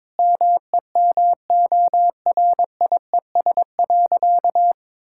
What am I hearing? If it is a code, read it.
Morse code, heard as a keyed tone, about 22 wpm, 700 Hz. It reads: MEMORIES.
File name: Difficult Sentences with Hints - 22wpm